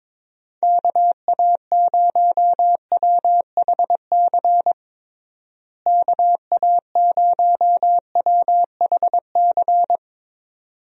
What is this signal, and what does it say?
Morse code, heard as a keyed tone, about 22 wpm, 700 Hz. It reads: KA0WHC KA0WHC
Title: USA Call Signs - 6 Characters - Repeated 2x - 22wpm